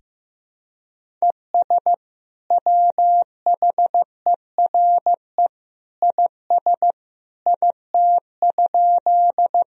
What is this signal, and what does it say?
Morse code, heard as a keyed tone, about 15 wpm, 700 Hz. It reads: ES WHERE IS IT?
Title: Sentences from Top 200 Words and Abbreviations 15wpm